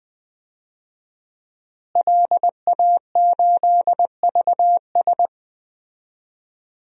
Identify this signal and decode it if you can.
Morse code, heard as a keyed tone, about 20 wpm, 700 Hz. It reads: LA8VS
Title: DXCC-100 Call Signs - 5 Characters - Repeated 3x - 20wpm